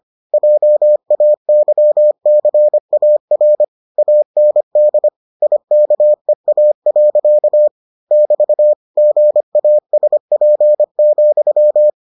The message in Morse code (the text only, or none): JAYCAR AND IKEA. <BT> GASP,